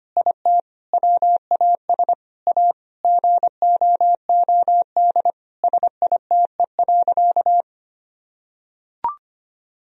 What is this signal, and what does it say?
Morse code, heard as a keyed tone, about 25 wpm, 700 Hz. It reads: IT WAS A GOOD SITE.
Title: Sentences from Top 1000 Words - Review Part 4 - 25wpm